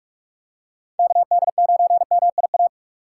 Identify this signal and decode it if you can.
Morse code, heard as a keyed tone, about 46 wpm, 700 Hz. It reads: KD9MIA
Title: USA Call Signs - 6 Characters - 6x Character Spacing - 45wpm